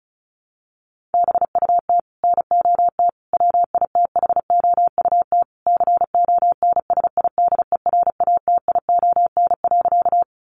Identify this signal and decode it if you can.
Morse code, heard as a keyed tone, about 34 wpm, 700 Hz. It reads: BUT NOT WITHOUT CONSIDERATION.
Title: Sentences from Top 1500 Words 35wpm